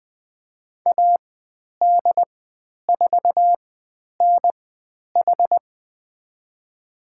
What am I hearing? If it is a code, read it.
Morse code, heard as a keyed tone, about 20 wpm, spaced as for 10 wpm, 700 Hz. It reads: AD4NH
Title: 095 Lesson 16g - Letter c - Call Signs 20wpm